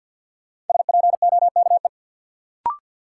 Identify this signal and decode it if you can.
Morse code, heard as a keyed tone, about 51 wpm, 700 Hz. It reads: SPOKE
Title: Sentences from Top 1200 Words - New Word Drill - 50wpm